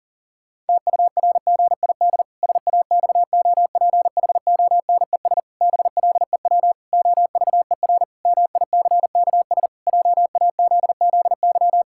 Morse code, heard as a keyed tone, about 40 wpm, 700 Hz. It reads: TURGID SAXOPHONES BLEW OVER MICKS JAZZY
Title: Pangram Challenge v2 - 40wpm